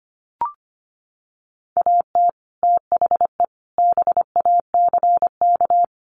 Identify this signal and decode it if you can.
Morse code, heard as a keyed tone, about 25 wpm, 700 Hz. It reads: AT THE BACK
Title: Sets of 5 Words 25wpm